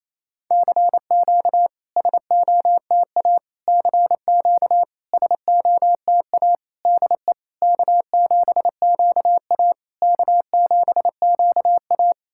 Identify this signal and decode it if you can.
Morse code, heard as a keyed tone, about 28 wpm, 690 Hz. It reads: CQ SOTA CQ SOTA DE K7QA K7QA
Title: SOTA QSO Practice 28wpm